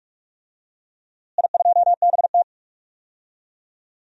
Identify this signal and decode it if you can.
Morse code, heard as a keyed tone, about 44 wpm, 700 Hz. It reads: I2BT
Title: DXCC-100 Call Signs - 4 Characters - Repeated 2x - 45wpm